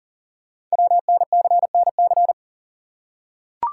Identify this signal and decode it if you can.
Morse code, heard as a keyed tone, about 38 wpm, 700 Hz. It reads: WNCNC
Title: Code Groups - 5 Letters - 40wpm